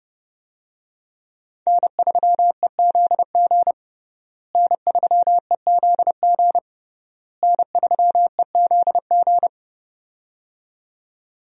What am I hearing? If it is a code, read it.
Morse code, heard as a keyed tone, about 30 wpm, 700 Hz. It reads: N3EZG N3EZG N3EZG
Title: USA Call Signs - 5 Characters - Repeated 3x - 30wpm